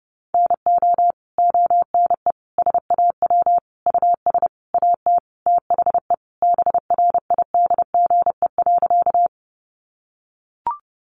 Morse code, heard as a keyed tone, about 30 wpm, 700 Hz. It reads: NO ONE SAW US AT THE BRIDGE.
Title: Sentences from Top 1200 Words 30wpm